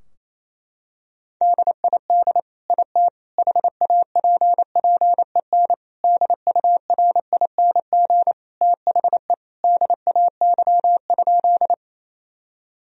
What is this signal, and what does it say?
Morse code, heard as a keyed tone, about 28 wpm, 700 Hz. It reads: DID IT HAPPEN DURING THE DAY?
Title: Sentences from Top 1000 Words - Review Part 2 - 28wpm